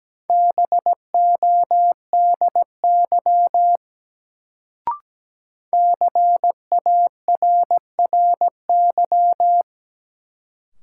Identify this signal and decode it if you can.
Morse code, heard as a keyed tone, about 17 wpm, 700 Hz. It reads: BODY CARRY
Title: Sentences from Top 400 Words - New Word Drill - 17wpm